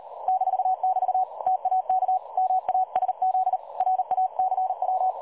Morse code, heard as a keyed tone, about 38 wpm, 750 Hz. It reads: -<BT> TAK MASZ RACW
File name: dlazaawansowanych / nagraniazpasma / websdr recording 3565.0kHz